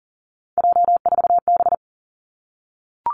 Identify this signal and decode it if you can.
Morse code, heard as a keyed tone, about 41 wpm, 700 Hz. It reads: J4B
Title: Three Challenging Letter-Number Combinations 40wpm